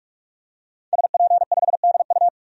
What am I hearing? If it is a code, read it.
Morse code, heard as a keyed tone, about 46 wpm, 700 Hz. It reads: SP5DU
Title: DXCC-100 Call Signs - 5 Characters - 45wpm